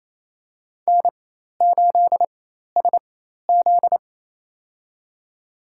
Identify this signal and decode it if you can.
Morse code, heard as a keyed tone, about 28 wpm, spaced as for 13 wpm, 700 Hz. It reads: N8SZ
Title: USA Call Signs - 4 Characters - 4x Character Spacing - 28wpm